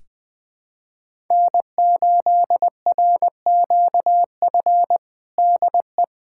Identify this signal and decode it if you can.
Morse code, heard as a keyed tone, about 21 wpm, 700 Hz. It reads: N8RQF DE
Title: USA Callsign DE Callsign - 5 Characters - 20wpm